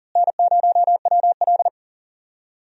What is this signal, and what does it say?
Morse code, heard as a keyed tone, about 40 wpm, 700 Hz. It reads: N0WL
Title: USA Call Signs - 4 Characters - Repeated 2x - 40wpm